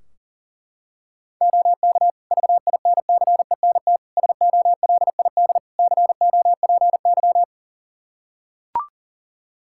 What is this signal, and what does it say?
Morse code, heard as a keyed tone, about 40 wpm, 700 Hz. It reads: OK VINCENT SOLID COPY
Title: Practice QSO - Exchange 3 - More Info - 40wpm